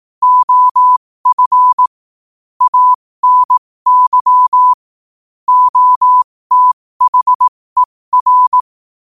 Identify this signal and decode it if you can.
Morse code, heard as a keyed tone, about 18 wpm, spaced as for 15 wpm, 1000 Hz. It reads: OF ANY OTHER
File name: qotdfiles / 15wpm